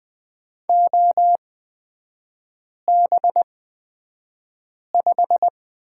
Morse code, heard as a keyed tone, about 20 wpm, spaced as for 5 wpm, 700 Hz. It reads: OB5
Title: 255 Lesson 41e - Period - Three Characters 20wpm